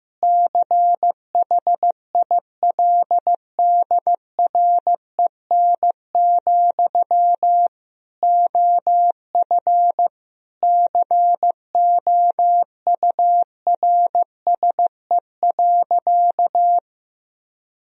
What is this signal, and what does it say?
Morse code, heard as a keyed tone, about 15 wpm, 700 Hz. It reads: CHILDREN, OF COURSE.
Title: Sentences from Top 1000 Words - Review Part 1 - 15wpm